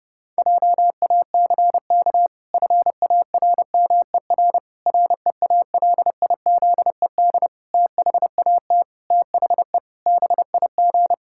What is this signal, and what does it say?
Morse code, heard as a keyed tone, about 32 wpm, 700 Hz. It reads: JACK FARMER REALIZED THAT THE BIG
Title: Pangram Challenge v2 - 30wpm